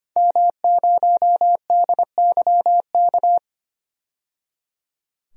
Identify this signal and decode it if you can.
Morse code, heard as a keyed tone, about 25 wpm, 700 Hz. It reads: M0DYK
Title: DXCC-100 Call Signs - 5 Characters - 25wpm